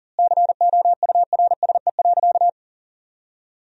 Morse code, heard as a keyed tone, about 40 wpm, 700 Hz. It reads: COURSE.
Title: 4x Word-Spacing - Sentences from Top 500 Words - 40wpm